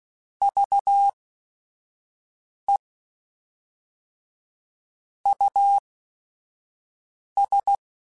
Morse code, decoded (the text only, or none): VE US